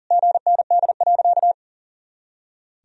Morse code, heard as a keyed tone, about 40 wpm, 700 Hz. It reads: GND.
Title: Sentences from Top 500 Words and Abbreviations 40wpm